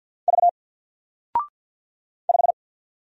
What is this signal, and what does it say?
Morse code, heard as a keyed tone, about 50 wpm, 700 Hz. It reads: V 5
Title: Single Character (Letter:Number:Punctuation:Prosign) Alt 50wpm